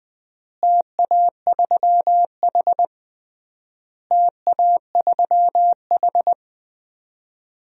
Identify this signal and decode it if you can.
Morse code, heard as a keyed tone, about 20 wpm, 690 Hz. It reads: TA3H TA3H
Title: DXCC-100 Call Signs - 4 Characters - Repeated 3x - 20wpm